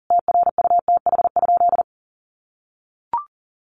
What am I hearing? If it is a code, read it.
Morse code, heard as a keyed tone, about 40 wpm, 700 Hz. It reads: TRUTH?
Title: Sentences from Top 800 Words 40wpm